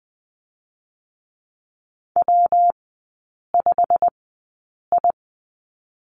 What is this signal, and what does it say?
Morse code, heard as a keyed tone, about 20 wpm, 700 Hz. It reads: W 5 I
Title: 114 Lesson 19e - Letter w - Three Characters 20wpm